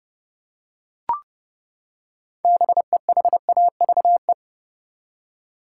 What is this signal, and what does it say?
Morse code, heard as a keyed tone, about 30 wpm, 700 Hz. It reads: BEHAVE
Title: Six Letter Words - 30wpm